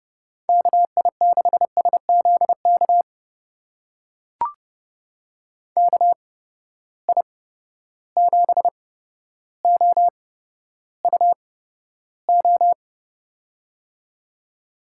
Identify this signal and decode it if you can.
Morse code, heard as a keyed tone, about 31 wpm, spaced as for 13 wpm, 700 Hz. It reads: KI6SZK KI7OUO
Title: USA Call Signs - 6 Characters - 8x Character Spacing - 30wpm